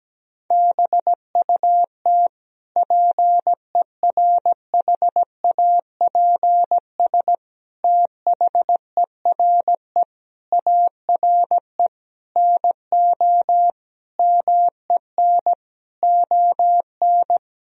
Morse code, heard as a keyed tone, about 17 wpm, 700 Hz. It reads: BUT PERHAPS THERE ARE NO MEN ON
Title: Sentences from Top 1000 Words - Review Part 3 - 17wpm